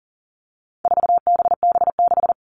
Morse code, heard as a keyed tone, about 40 wpm, 700 Hz. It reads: 4BB6
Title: All Cut Numbers - 4-Digit 40wpm